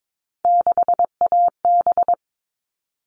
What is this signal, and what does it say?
Morse code, heard as a keyed tone, about 22 wpm, 700 Hz. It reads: 6AB
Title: All Cut Numbers - 3-Digit 22wpm